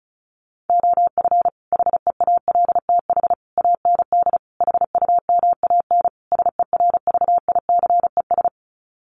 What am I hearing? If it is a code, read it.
Morse code, heard as a keyed tone, about 34 wpm, 700 Hz. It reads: OF HEALTH AND HUMAN SERVICES
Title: Sets of 5 Words 35wpm